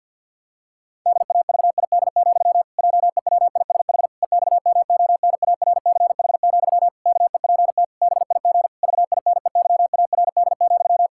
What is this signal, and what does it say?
Morse code, heard as a keyed tone, about 50 wpm, 700 Hz. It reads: DAVID, JEWISH EXMONARCH, KEPT BIG VINEYARD,
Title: Pangram Challenge v2 - 50wpm